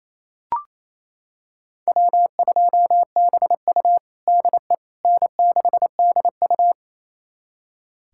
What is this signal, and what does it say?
Morse code, heard as a keyed tone, about 28 wpm, 700 Hz. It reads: W2BU DE N6DU
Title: USA Callsign DE Callsign - 4 Characters - 28wpm